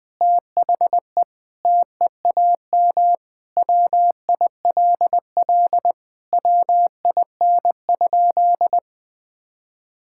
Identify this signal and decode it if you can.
Morse code, heard as a keyed tone, about 20 wpm, 700 Hz. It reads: THE TEAM WILL WIN?
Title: Sentences with 4-Word Phrases - 20wpm